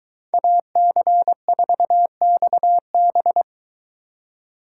Morse code, heard as a keyed tone, about 22 wpm, 700 Hz. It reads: AC4XB